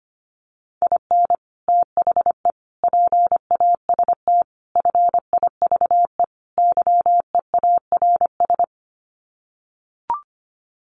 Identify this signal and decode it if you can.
Morse code, heard as a keyed tone, about 25 wpm, 690 Hz. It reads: IN THE PAST FIVE YEARS
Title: Sets of 5 Words Encore 25wpm